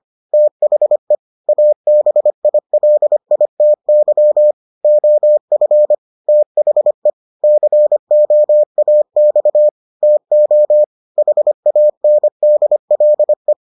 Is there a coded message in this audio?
Yes: Morse code from 0.3 to 13.5 s, heard as a keyed tone, about 25 wpm, 600 Hz. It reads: THE ABILITY OF THE COAX TO HANDLE